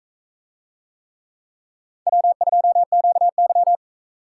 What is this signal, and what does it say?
Morse code, heard as a keyed tone, about 44 wpm, 700 Hz. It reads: W2QY